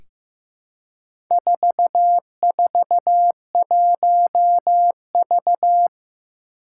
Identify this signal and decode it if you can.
Morse code, heard as a keyed tone, about 15 wpm, 700 Hz. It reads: 441V
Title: Four Challenging Letter-Number Combinations 15wpm